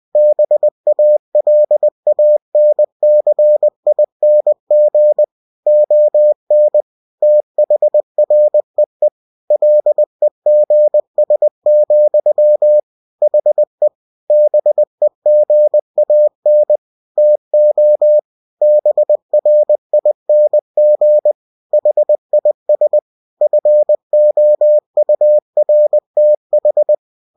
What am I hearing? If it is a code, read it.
Morse code, heard as a keyed tone, about 20 wpm, 600 Hz. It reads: BALANCING ON THREE LEGS, HE BEGAN TO BRING HIS FOURTH
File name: cw-books / winnie-the-pooh / winnie-20wpm-page-0054